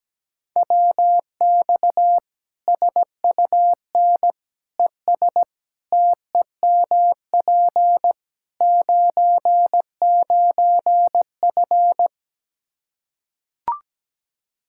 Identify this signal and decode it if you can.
Morse code, heard as a keyed tone, about 17 wpm, 700 Hz. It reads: WX SUN ES TEMP 99F
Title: Practice QSO - Exchange 2 - Info - 17wpm